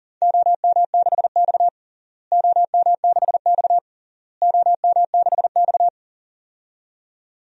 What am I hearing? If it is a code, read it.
Morse code, heard as a keyed tone, about 40 wpm, 700 Hz. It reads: OM6X OM6X OM6X
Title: DXCC-100 Call Signs - 4 Characters - Repeated 3x - 40wpm